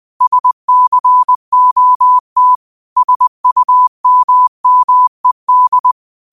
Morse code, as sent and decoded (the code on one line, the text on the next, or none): ... -.-. --- - / ... ..- -- -- . -..
SCOT SUMMED